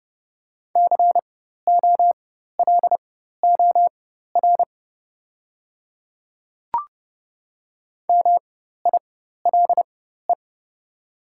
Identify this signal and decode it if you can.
Morse code, heard as a keyed tone, about 30 wpm, spaced as for 14 wpm, 690 Hz. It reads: COLOR MILE
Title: 4x Character Spacing - Top 300 Words - 30wpm